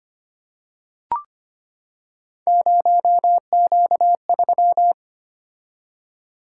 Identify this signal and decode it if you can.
Morse code, heard as a keyed tone, about 25 wpm, 700 Hz. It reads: 0Q3
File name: Three Letter-Number Combinations - 25wpm